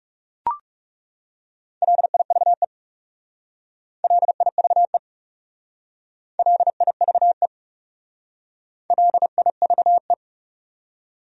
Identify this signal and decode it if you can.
Morse code, heard as a keyed tone, about 37 wpm, 700 Hz. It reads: LIVE LIVE LIVE LIVE
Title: Speed-Racing Top 300 Words 30wpm